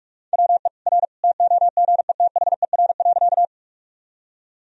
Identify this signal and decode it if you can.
Morse code, heard as a keyed tone, about 46 wpm, 700 Hz. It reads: WE R TOGETHER.
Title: Sentences from Top 300 Words and Abbreviations 45wpm